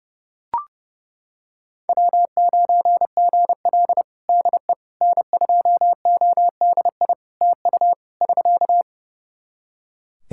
Morse code, heard as a keyed tone, about 30 wpm, 700 Hz. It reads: W9GL DE N2ODI TU <SK>
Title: Practice QSO - Exchange 4 - Ending - 30wpm